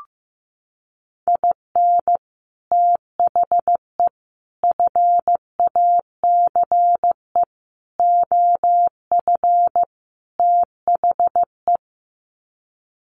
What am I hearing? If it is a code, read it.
Morse code, heard as a keyed tone, about 15 wpm, 700 Hz. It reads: IN THE FACE OF THE
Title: Sets of 5 Words 15wpm